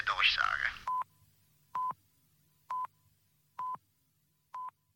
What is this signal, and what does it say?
Speech. There is no code.